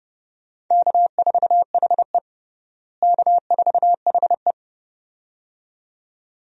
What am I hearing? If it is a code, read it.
Morse code, heard as a keyed tone, about 30 wpm, 700 Hz. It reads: K4HE K4HE
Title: USA Call Signs - 4 Characters - Repeated 2x - 30wpm